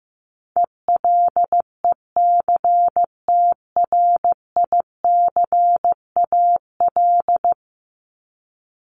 Morse code, heard as a keyed tone, about 15 wpm, 700 Hz. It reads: ELECTRICAL
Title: Top 751-1000 Words 15wpm